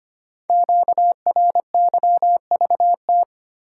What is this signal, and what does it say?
Morse code, heard as a keyed tone, about 25 wpm, 690 Hz. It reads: QRYVT
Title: Code Groups - 5 Letter:Numbers - 25wpm